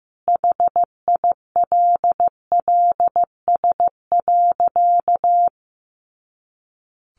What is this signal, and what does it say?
Morse code, heard as a keyed tone, about 14 wpm, 700 Hz. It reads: HILLS.
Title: A Sound Of Thunder 15wpm